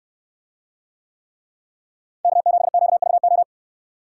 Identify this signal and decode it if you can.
Morse code, heard as a keyed tone, about 65 wpm, 700 Hz. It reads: K7QVO